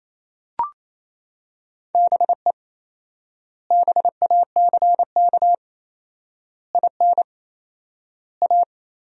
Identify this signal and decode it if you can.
Morse code, heard as a keyed tone, about 28 wpm, 700 Hz. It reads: BE BACK IN A
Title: 4x Word-Spacing - Sentences from Top 500 Words - 28wpm